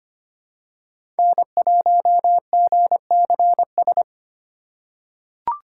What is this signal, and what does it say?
Morse code, heard as a keyed tone, about 25 wpm, 700 Hz. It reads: N1GCS
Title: USA Call Signs - 5 Characters - 2x Character Spacing - 25wpm